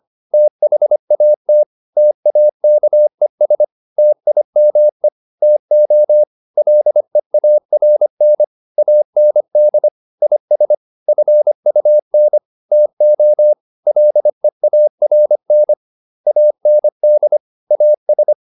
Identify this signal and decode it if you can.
Morse code, heard as a keyed tone, about 25 wpm, 600 Hz. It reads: THAT TAKES TIME TO LEARN AND IS FUN TO LEARN AND AS